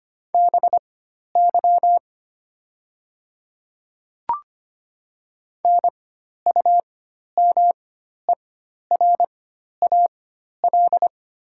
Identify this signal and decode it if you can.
Morse code, heard as a keyed tone, about 25 wpm, spaced as for 12 wpm, 700 Hz. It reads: BY NUMERAL
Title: 4x Character Spacing - Top 500 Words - 25wpm